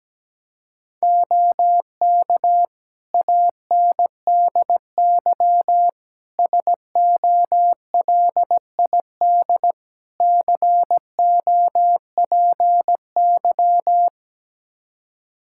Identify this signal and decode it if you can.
Morse code, heard as a keyed tone, about 17 wpm, 700 Hz. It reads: OK ANDY SOLID COPY
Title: K1USN SST QSO Practice 17wpm